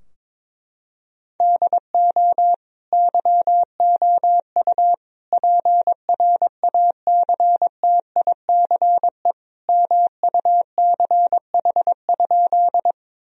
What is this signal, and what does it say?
Morse code, heard as a keyed tone, about 21 wpm, 700 Hz. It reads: DO YOU PRACTICE MUCH?
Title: Sentences from Top 1000 Words - Review Part 4 - 22wpm